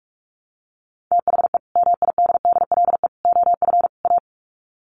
Morse code, heard as a keyed tone, about 44 wpm, 700 Hz. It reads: THE MIDDLE OF A
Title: Sets of 4 Words 45wpm